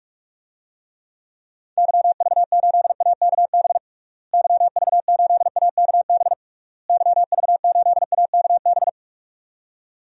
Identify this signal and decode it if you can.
Morse code, heard as a keyed tone, about 46 wpm, 700 Hz. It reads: YV8AKB YV8AKB YV8AKB
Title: DXCC-100 Call Signs - 6 Characters - Repeated 3x - 45wpm